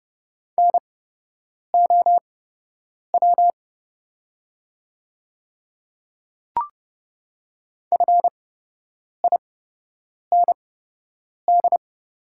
Morse code, heard as a keyed tone, about 30 wpm, spaced as for 8 wpm, 690 Hz. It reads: NOW FIND
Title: 8x Character Spacing - Top 200 Words - 30wpm